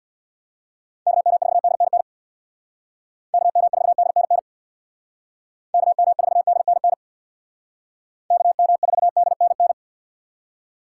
Morse code, heard as a keyed tone, about 61 wpm, 700 Hz. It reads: KM4DNN KM4DNN KM4DNN KM4DNN
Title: Speed Racing - USA Call Signs 50wpm